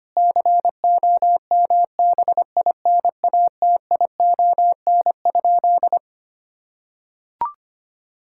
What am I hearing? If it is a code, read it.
Morse code, heard as a keyed tone, about 25 wpm, 700 Hz. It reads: COMBINATION?
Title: Sentences from Top 1000 Words - Review Part 6 - 25wpm